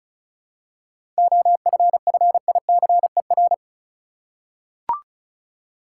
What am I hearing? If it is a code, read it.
Morse code, heard as a keyed tone, about 36 wpm, 700 Hz. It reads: OFFICER
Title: Seven Letter Words - 35wpm